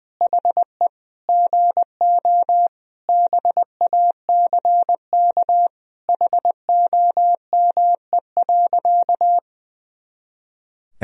Morse code, heard as a keyed tone, about 20 wpm, 700 Hz. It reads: HE GO BACK HOME.
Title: Sentences from Top 500 Words and Abbreviations 20wpm